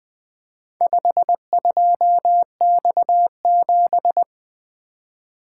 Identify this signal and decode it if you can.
Morse code, heard as a keyed tone, about 20 wpm, 700 Hz. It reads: 52X7
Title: Four Letter-Number Combinations - 20wpm